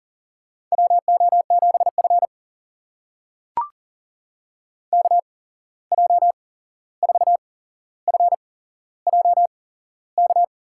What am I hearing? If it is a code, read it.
Morse code, heard as a keyed tone, about 38 wpm, spaced as for 15 wpm, 700 Hz. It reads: WO7F KJ4FJK
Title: USA Call Signs - 8x Character Spacing - 40wpm